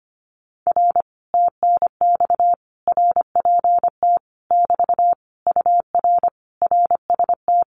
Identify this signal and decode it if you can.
Morse code, heard as a keyed tone, about 25 wpm, 700 Hz. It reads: R TNX RPT <BT> UR RST
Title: Common QSO Phrases Encore 25wpm